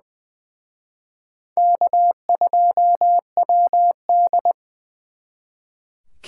Morse code, heard as a keyed tone, about 24 wpm, 700 Hz. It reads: K2WD
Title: Speed Racing - USA 1x2 Call Signs 20wpm